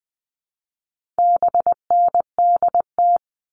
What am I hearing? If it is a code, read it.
Morse code, heard as a keyed tone, about 20 wpm, 700 Hz. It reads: BNDT
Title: All Cut Numbers - 4-Digit 20wpm